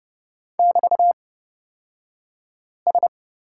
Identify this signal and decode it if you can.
Morse code, heard as a keyed tone, about 31 wpm, spaced as for 5 wpm, 700 Hz. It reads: <BT>S
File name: Lesson 36d - BT - Two Characters 30wpm